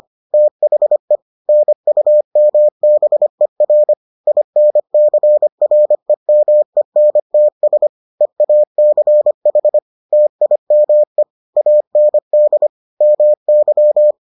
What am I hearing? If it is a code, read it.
Morse code, heard as a keyed tone, about 25 wpm, 600 Hz. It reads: THE NUMBER INCREMENTS EACH TIME AND MY